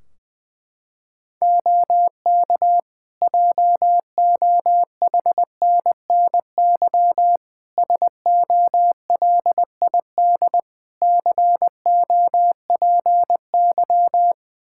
Morse code, heard as a keyed tone, about 20 wpm, 700 Hz. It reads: OK JOHNNY SOLID COPY
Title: Practice QSO - Exchange 3 - More Info - 20wpm